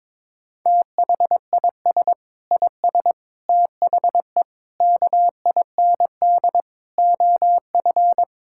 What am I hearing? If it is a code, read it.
Morse code, heard as a keyed tone, about 22 wpm, 700 Hz. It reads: THIS IS THE KIND OF
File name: practice / 22wpm / Sets of 7 Words - 22wpm